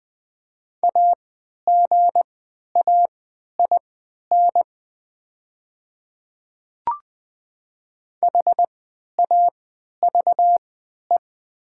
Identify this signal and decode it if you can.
Morse code, heard as a keyed tone, about 20 wpm, spaced as for 11 wpm, 700 Hz. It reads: AGAIN HAVE
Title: 3x Character Spacing - Top 300 Words - 20wpm